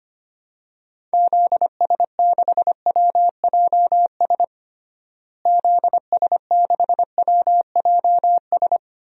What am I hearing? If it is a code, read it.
Morse code, heard as a keyed tone, about 25 wpm, 700 Hz. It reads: ZS6WJS ZS6WJS
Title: DXCC-100 Call Signs - 6 Characters - Repeated 3x - 25wpm